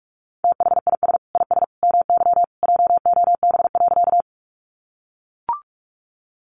Morse code, heard as a keyed tone, about 44 wpm, 700 Hz. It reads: THIS IS MY JOB.
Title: Sentences from Top 600 Words 45wpm